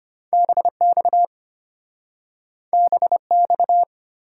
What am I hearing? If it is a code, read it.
Morse code, heard as a keyed tone, about 34 wpm, 700 Hz. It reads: BX BX